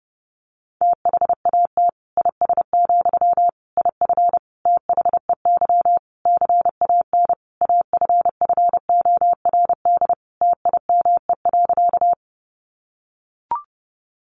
Morse code, heard as a keyed tone, about 30 wpm, 700 Hz. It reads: THAT IS, IF THEY CAN AFFORD TIME.
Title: Sentences from Top 1100 Words 30wpm